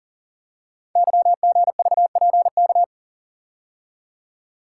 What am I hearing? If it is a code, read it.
Morse code, heard as a keyed tone, about 39 wpm, 690 Hz. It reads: YGVPK
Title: Code Groups - 5 Letters - 40wpm